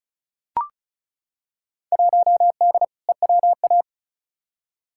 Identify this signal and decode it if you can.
Morse code, heard as a keyed tone, about 36 wpm, 700 Hz. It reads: 1D EWA
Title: Field Day Exchanges - 35wpm